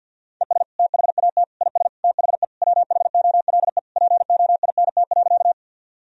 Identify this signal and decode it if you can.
Morse code, heard as a keyed tone, about 51 wpm, 700 Hz. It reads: ES THAT IS THE WHOLE POINT.
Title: Sentences from Top 400 Words and Abbreviations 50wpm